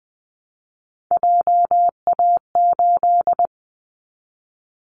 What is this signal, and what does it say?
Morse code, heard as a keyed tone, about 20 wpm, 700 Hz. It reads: JA8
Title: Callsign Prefixes for Top 50 DXCC - 20wpm